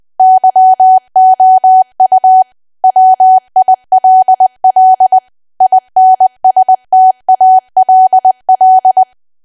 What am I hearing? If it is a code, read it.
Morse code, heard as a keyed tone, about 20 wpm, 750 Hz. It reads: YOU WILL INSTALL
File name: cwa / W1AW-Level3 / 20WPM